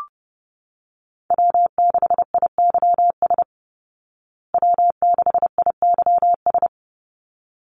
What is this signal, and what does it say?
Morse code, heard as a keyed tone, about 30 wpm, 700 Hz. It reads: W6IYS W6IYS
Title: Call Signs Repeated 3x - 30wpm